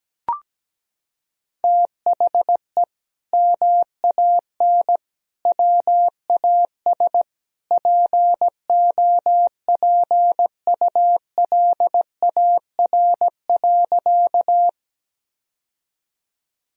Morse code, heard as a keyed tone, about 17 wpm, 700 Hz. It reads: THE MAN WAS POPULAR.
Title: Sentences from Top 1000 Words - Review Part 3 - 17wpm